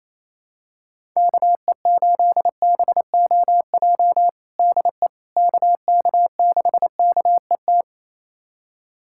Morse code, heard as a keyed tone, about 28 wpm, 700 Hz. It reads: KE8BOJ DE KK6KET
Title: USA Callsign DE Callsign - 6 Characters - 28wpm